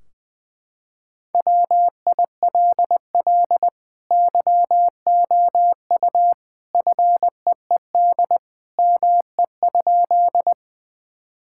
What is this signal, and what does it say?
Morse code, heard as a keyed tone, about 20 wpm, 700 Hz. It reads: WILL YOU FEED ME?
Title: Sentences from Top 1600 Words 20wpm